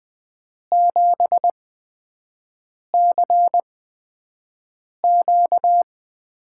Morse code, heard as a keyed tone, about 20 wpm, spaced as for 5 wpm, 700 Hz. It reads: 7CQ
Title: USA Call Signs - 6 Characters - 8x Character Spacing - 20wpm